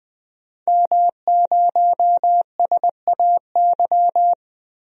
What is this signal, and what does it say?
Morse code, heard as a keyed tone, about 20 wpm, 700 Hz. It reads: M0SAY
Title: DXCC-100 Call Signs - 5 Characters - 20wpm